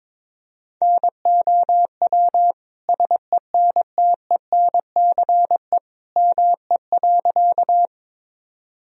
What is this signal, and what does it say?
Morse code, heard as a keyed tone, about 23 wpm, 700 Hz. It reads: NOW SENTENCE ME.
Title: Sentences from Top 1000 Words - Review Part 1 - 22wpm